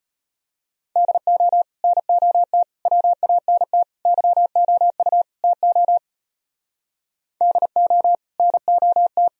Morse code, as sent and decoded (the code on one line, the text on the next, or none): -.. --- / -. --- - / .-- .- -. - / -.-- --- ..- / - --- / -.. --- / -. --- -
DO NOT WANT YOU TO DO NOT